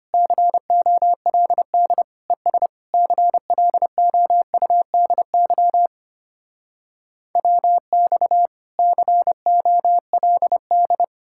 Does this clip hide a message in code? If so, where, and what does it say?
Yes: Morse code from 0.1 to 11.0 s, heard as a keyed tone, about 33 wpm, 700 Hz. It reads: COLD ES CLOUDY WX COLD